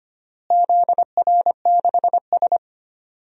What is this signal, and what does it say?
Morse code, heard as a keyed tone, about 25 wpm, 700 Hz. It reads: ZR6S
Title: DXCC-100 Call Signs - 4 Characters - Repeated 3x - 25wpm